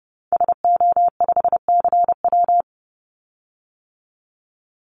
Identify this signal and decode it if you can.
Morse code, heard as a keyed tone, about 30 wpm, 700 Hz. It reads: SO5CW
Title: Call Signs - 30wpm